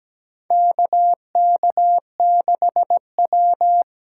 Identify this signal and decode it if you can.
Morse code, heard as a keyed tone, about 17 wpm, 700 Hz. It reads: KK6W
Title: USA Call Signs - 8x Character Spacing - 17wpm